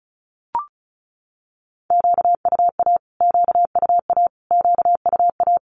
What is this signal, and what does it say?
Morse code, heard as a keyed tone, about 34 wpm, 700 Hz. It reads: QUA QUA QUA
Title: Q-Codes Repeated 3x 35wpm